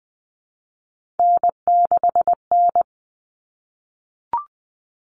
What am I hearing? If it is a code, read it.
Morse code, heard as a keyed tone, about 20 wpm, 700 Hz. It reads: N6N
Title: All Cut Numbers - 3-Digit 20wpm